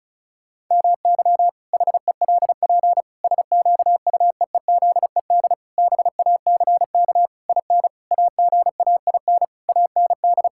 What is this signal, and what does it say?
Morse code, heard as a keyed tone, about 36 wpm, 700 Hz. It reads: MY HELP SQUEEZED BACK IN AGAIN AND
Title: Pangram Challenge v2 - 35wpm